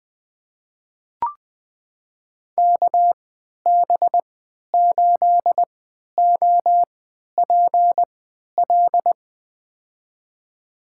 Morse code, heard as a keyed tone, about 20 wpm, spaced as for 11 wpm, 700 Hz. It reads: KB8OPL
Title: USA Call Signs - 6 Characters - 3x Character Spacing - 20wpm